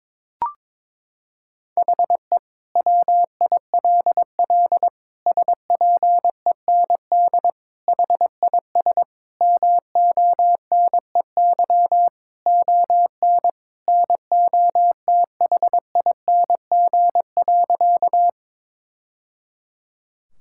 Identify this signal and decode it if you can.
Morse code, heard as a keyed tone, about 22 wpm, 700 Hz. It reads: HE WILL SPEND HIS MONEY ON NOTHING.
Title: Sentences from Top 1000 Words - Review Part 5 - 22wpm